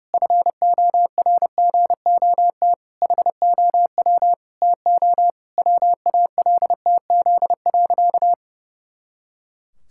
Morse code, heard as a keyed tone, about 32 wpm, 700 Hz. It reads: FORGOT HOW TO WALTZ.
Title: Pangram Challenge v2 - 30wpm